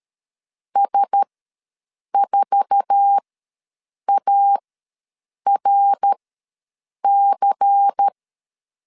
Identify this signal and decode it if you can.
Morse code, heard as a keyed tone, about 13 wpm, spaced as for 7 wpm, 800 Hz. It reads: S4ARC
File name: LEZIONI / 9 2 ws 12wpm 3406